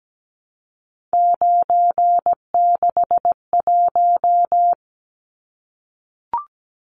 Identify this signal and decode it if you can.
Morse code, heard as a keyed tone, about 17 wpm, 700 Hz. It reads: 961
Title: Numbers - 3-Digit 17wpm